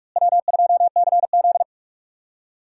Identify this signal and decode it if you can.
Morse code, heard as a keyed tone, about 46 wpm, 700 Hz. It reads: W2CZ
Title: USA Call Signs - 4 Characters - Repeated 2x - 45wpm